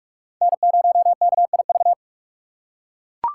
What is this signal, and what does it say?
Morse code, heard as a keyed tone, about 43 wpm, 700 Hz. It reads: N0KIV
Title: USA Call Signs - 5 Characters - 6x Character Spacing - 45wpm